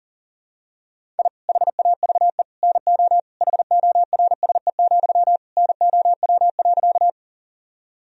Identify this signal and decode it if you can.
Morse code, heard as a keyed tone, about 37 wpm, 700 Hz. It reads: I HAVE NO HORSE, NOW.
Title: Sentences from Top 1000 Words - Review Part 1 - 40wpm